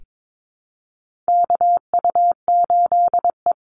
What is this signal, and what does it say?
Morse code, heard as a keyed tone, about 22 wpm, 700 Hz. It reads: KU8E
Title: Call Signs Repeated 3x - 22wpm